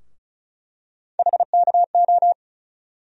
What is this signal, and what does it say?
Morse code, heard as a keyed tone, about 34 wpm, 700 Hz. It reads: HKO
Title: USA Call Sign Postfixes - 3 Letters 35wpm